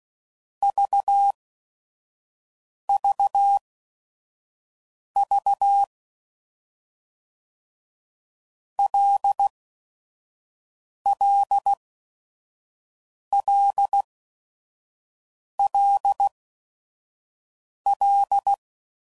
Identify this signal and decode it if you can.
Morse code, heard as a keyed tone, about 16 wpm, spaced as for 5 wpm, 800 Hz. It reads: VVV LLLLL
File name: Basic Letter Groups By7Groups / LFCYQ / LFCYQ@5 f15